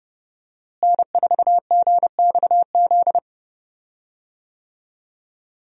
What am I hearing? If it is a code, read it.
Morse code, heard as a keyed tone, about 30 wpm, 700 Hz. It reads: N4GXZ
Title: USA Call Signs - 5 Characters - Repeated 3x - 30wpm